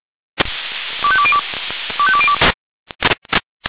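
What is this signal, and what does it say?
Five-tone selective call (ZVEI), heard as quick runs of five tones, tones of about 70 ms. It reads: 25202 25202
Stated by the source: ZVEI five-tone selective call received on SDR.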